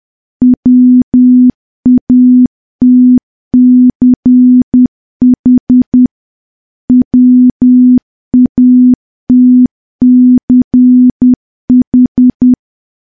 Morse code, heard as a keyed tone, about 10 wpm, 250 Hz. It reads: WATCH WATCH
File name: morse / morse-training-10WPM